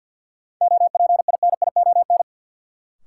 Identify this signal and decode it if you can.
Morse code, heard as a keyed tone, about 52 wpm, 700 Hz. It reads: OPINION
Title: Sentences from Top 1200 Words - New Word Drill Alt - 50wpm